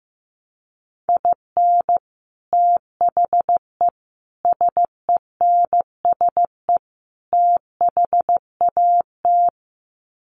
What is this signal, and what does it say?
Morse code, heard as a keyed tone, about 15 wpm, 700 Hz. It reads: IN THE SENSE THAT
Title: Sets of 4 Words 15wpm